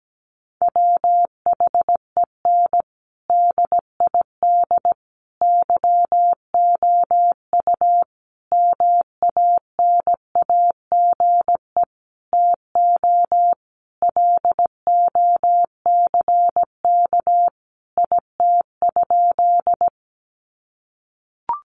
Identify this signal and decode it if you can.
Morse code, heard as a keyed tone, about 17 wpm, 700 Hz. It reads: WHEN DID YOU MANAGE TO LOCK IT?
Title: Sentences from Top 1400 Words 17wpm